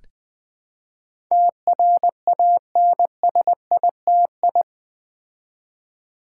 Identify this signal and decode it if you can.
Morse code, heard as a keyed tone, about 20 wpm, 700 Hz. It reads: TRANSITI
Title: Build A Word - 4 Letters At A Time - 10-Letter Words - 20wpm